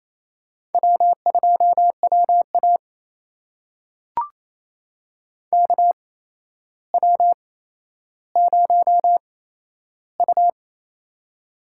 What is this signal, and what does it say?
Morse code, heard as a keyed tone, about 28 wpm, spaced as for 12 wpm, 700 Hz. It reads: W2WA KW0U
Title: USA Call Signs - 4 Characters - 8x Character Spacing - 28wpm